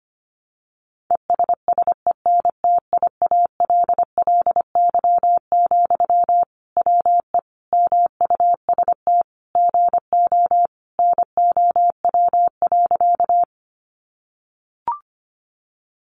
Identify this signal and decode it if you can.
Morse code, heard as a keyed tone, about 25 wpm, 700 Hz. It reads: ESSENTIALLY, WE MUST GO NOW.
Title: Sentences from Top 1300 Words 25wpm